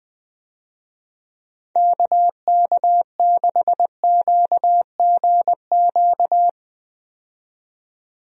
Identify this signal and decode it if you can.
Morse code, heard as a keyed tone, about 20 wpm, 700 Hz. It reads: KK6QGQ
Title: USA Call Signs - 6 Characters - 20wpm